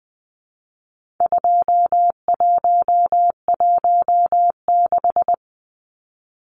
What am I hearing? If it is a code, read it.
Morse code, heard as a keyed tone, about 20 wpm, 700 Hz. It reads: 2116
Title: Numbers - 4-Digit 20wpm